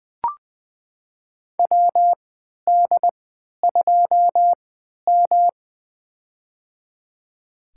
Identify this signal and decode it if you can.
Morse code, heard as a keyed tone, about 20 wpm, spaced as for 11 wpm, 700 Hz. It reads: WD2M
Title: USA Call Signs - 4 Characters - 3x Character Spacing - 20wpm